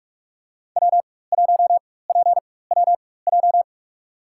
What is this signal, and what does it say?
Morse code, heard as a keyed tone, about 43 wpm, spaced as for 21 wpm, 700 Hz. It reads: W1PWJ
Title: USA Call Signs - 5 Characters - 4x Character Spacing - 45wpm